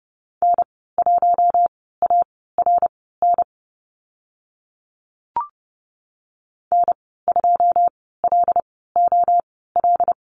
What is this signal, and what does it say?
Morse code, heard as a keyed tone, about 30 wpm, spaced as for 17 wpm, 700 Hz. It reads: N1ARN N2LOL
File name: Lesson 14g - Number 2 - Call Signs 30wpm